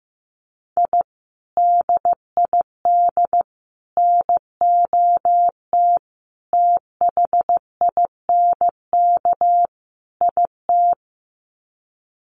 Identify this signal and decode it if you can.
Morse code, heard as a keyed tone, about 15 wpm, 700 Hz. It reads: I DID NOT THINK IT
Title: Sets of 5 Words Encore2 15wpm